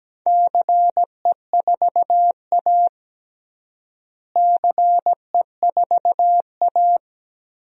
Morse code, heard as a keyed tone, about 17 wpm, 700 Hz. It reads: CE4A CE4A
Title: DXCC-100 Call Signs - 4 Characters - Repeated 2x - 17wpm